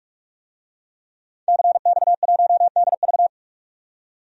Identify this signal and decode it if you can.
Morse code, heard as a keyed tone, about 46 wpm, 700 Hz. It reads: CX1DV
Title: DXCC-100 Call Signs - 5 Characters - 45wpm